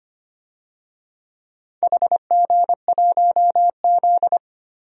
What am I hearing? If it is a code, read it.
Morse code, heard as a keyed tone, about 25 wpm, 700 Hz. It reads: HG1Z
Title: DXCC-100 Call Signs - 4 Characters - 25wpm